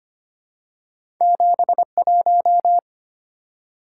Morse code, heard as a keyed tone, about 25 wpm, 700 Hz. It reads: 71
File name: Two Letter-Number Combinations - 25wpm